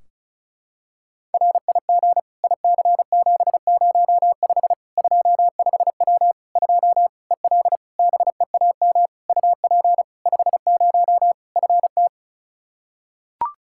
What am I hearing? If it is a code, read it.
Morse code, heard as a keyed tone, about 36 wpm, 700 Hz. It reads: RIG IC705 25W 2 EL BEAM UP 50 FT
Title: Practice QSO - Exchange 2 - Info - 35wpm